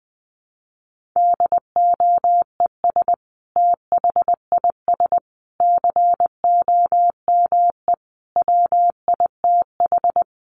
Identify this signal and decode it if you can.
Morse code, heard as a keyed tone, about 20 wpm, 700 Hz. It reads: DOES THIS COME WITH
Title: Sentences from Top 1400 Words 20wpm